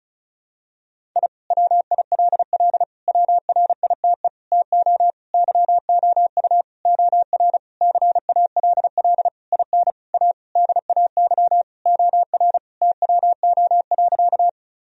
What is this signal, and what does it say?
Morse code, heard as a keyed tone, about 36 wpm, 700 Hz. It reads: I WILL WRITE TO YOU OR CALL IN A DAY OR TWO.
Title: Sentences from Top 1000 Words - Review Part 1 - 35wpm